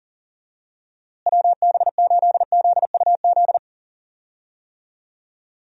Morse code, heard as a keyed tone, about 40 wpm, 700 Hz. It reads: WB8ZUZ
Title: USA Call Signs - 6 Characters - Repeated 2x - 40wpm